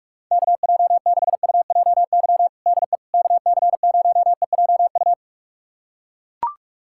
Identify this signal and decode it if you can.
Morse code, heard as a keyed tone, about 41 wpm, 700 Hz. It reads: KJ6UJY DE KC0EJU
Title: USA Callsign DE Callsign - 6 Characters - 45wpm